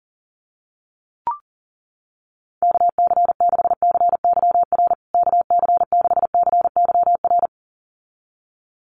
Morse code, heard as a keyed tone, about 40 wpm, 700 Hz. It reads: KC6CYR KC6CYR
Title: USA Call Signs Repeated 2x 40wpm